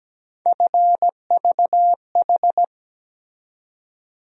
Morse code, heard as a keyed tone, about 17 wpm, 700 Hz. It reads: FVH
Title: USA Call Sign Postfixes - 3 Letters 17wpm